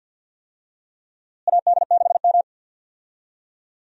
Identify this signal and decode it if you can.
Morse code, heard as a keyed tone, about 52 wpm, 700 Hz. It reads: AD6M
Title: USA Call Signs - 4 Characters - Repeated 2x - 50wpm